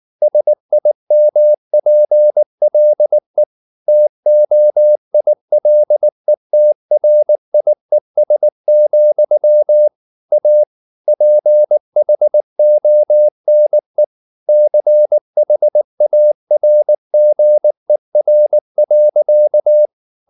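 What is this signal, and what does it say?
Morse code, heard as a keyed tone, about 19 wpm, 600 Hz. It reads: SIMPLE TOILETRIES, A PHONE CHARGER.
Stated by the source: Morse code at 19 wpm from CW med Gnister, reading Ken Follett's novel Never in English.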